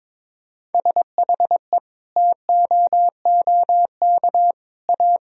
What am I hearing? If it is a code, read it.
Morse code, heard as a keyed tone, about 22 wpm, 700 Hz. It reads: SHE TOOK A
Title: Sentences from Top 1000 Words - Review Part 2 - 22wpm